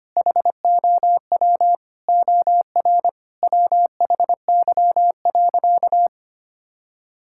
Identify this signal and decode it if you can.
Morse code, heard as a keyed tone, about 25 wpm, 700 Hz. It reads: HOW OR WHY.
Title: Sentences from Top 1000 Words - Review Part 6 - 25wpm